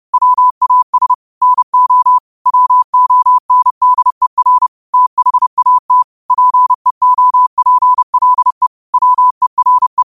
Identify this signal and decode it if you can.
Morse code, heard as a keyed tone, about 30 wpm, 1000 Hz. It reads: WAS NO WONDER THAT PEOPLE WERE